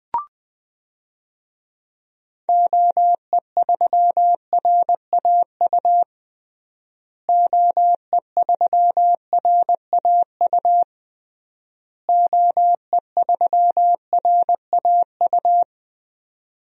Morse code, heard as a keyed tone, about 20 wpm, 700 Hz. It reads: OE3RAU OE3RAU OE3RAU
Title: DXCC-100 Call Signs - 6 Characters - Repeated 3x - 20wpm